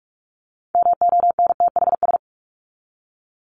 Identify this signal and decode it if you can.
Morse code, heard as a keyed tone, about 44 wpm, 700 Hz. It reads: MONTHS
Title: ES:S Suffix - Top 100 Words with Base Words 45wpm